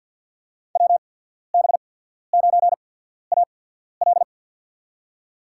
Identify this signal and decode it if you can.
Morse code, heard as a keyed tone, about 53 wpm, spaced as for 14 wpm, 700 Hz. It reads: WB9AL
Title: USA Call Signs - 5 Characters - 8x Character Spacing - 50wpm